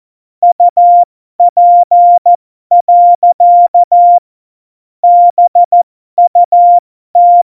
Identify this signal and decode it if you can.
Morse code, heard as a keyed tone, about 14 wpm, spaced as for 12 wpm, 700 Hz. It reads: UP. BUT
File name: Morsekurs / Witze